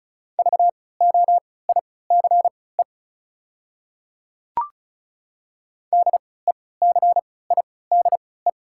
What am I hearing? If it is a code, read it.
Morse code, heard as a keyed tone, about 34 wpm, spaced as for 20 wpm, 690 Hz. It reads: VOICE DECIDE
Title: 3x Character Spacing - Top 500 Words - 35wpm